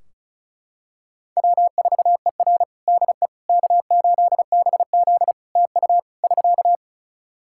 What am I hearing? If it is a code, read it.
Morse code, heard as a keyed tone, about 36 wpm, 700 Hz. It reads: W4ER DE K8BZ TU <SK>
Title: Practice QSO - Exchange 4 - Ending - 35wpm